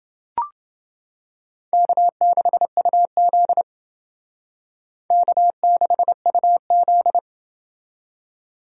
Morse code, heard as a keyed tone, about 29 wpm, 700 Hz. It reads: K6UZ K6UZ